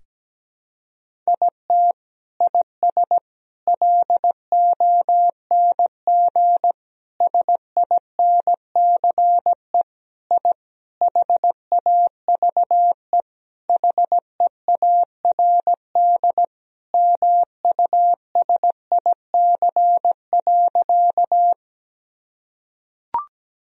Morse code, heard as a keyed tone, about 17 wpm, 700 Hz. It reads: IT IS LONG SINCE I HAVE HEARD MUSIC.
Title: Sentences from Top 1000 Words - Review Part 1 - 17wpm